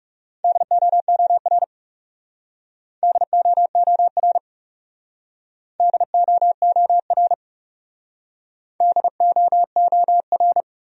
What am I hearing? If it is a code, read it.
Morse code, heard as a keyed tone, about 37 wpm, 700 Hz. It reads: DOOR DOOR DOOR DOOR
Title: Speed-Racing Top 500 Words 30wpm